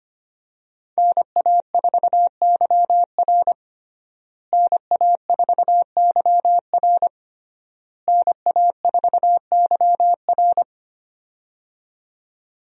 Morse code, heard as a keyed tone, about 25 wpm, 700 Hz. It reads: NA4YR NA4YR NA4YR
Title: USA Call Signs - 5 Characters - Repeated 3x - 25wpm